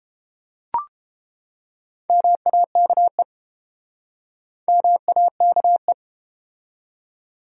Morse code, heard as a keyed tone, about 30 wpm, 700 Hz. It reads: MAKE MAKE